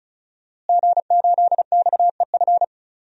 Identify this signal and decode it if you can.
Morse code, heard as a keyed tone, about 36 wpm, 700 Hz. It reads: G8XEF
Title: DXCC-100 Call Signs - 5 Characters - Repeated 2x - 35wpm